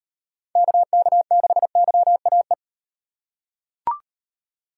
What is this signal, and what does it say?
Morse code, heard as a keyed tone, about 38 wpm, 700 Hz. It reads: KK6YAE